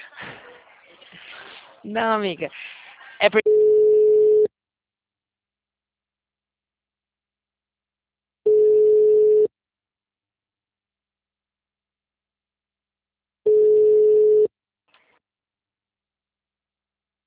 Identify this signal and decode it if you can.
Ringback tone (Europe, 425 Hz), with speech.